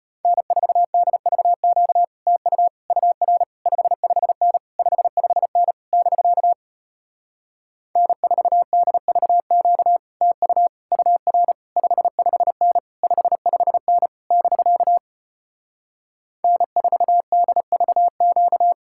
Morse code, heard as a keyed tone, about 33 wpm, 700 Hz. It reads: N4DVQ TU UR 55N 55N <BK> N4DVQ TU UR 55N 55N <BK> N4DVQ